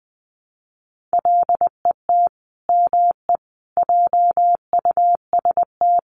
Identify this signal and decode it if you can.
Morse code, heard as a keyed tone, about 20 wpm, 700 Hz. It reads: LET ME JUST
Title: Sets of 5 Words Encore2 20wpm